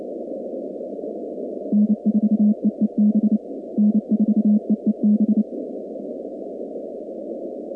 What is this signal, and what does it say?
Morse code, heard as a keyed tone, about 29 wpm, 210 Hz. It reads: N4EEB N4EEB
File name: contests / 2019-cw-wpx-kt5j-k5pi- -2330 0-46